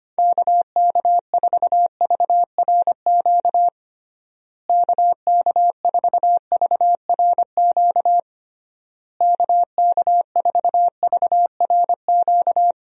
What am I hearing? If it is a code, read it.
Morse code, heard as a keyed tone, about 25 wpm, 700 Hz. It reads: KK4VRQ KK4VRQ KK4VRQ
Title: USA Call Signs - 6 Characters - Repeated 3x - 25wpm